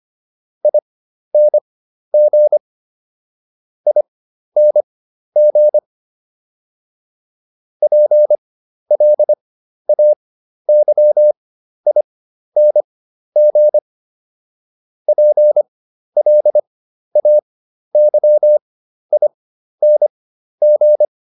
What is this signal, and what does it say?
Morse code, heard as a keyed tone, about 25 wpm, spaced as for 12 wpm, 600 Hz. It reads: ING ING PLAYING PLAYING